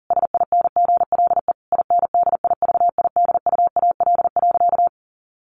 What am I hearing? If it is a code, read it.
Morse code, heard as a keyed tone, about 37 wpm, 700 Hz. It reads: SINGLE INDIVIDUAL.
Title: Sentences from Top 1500 Words 40wpm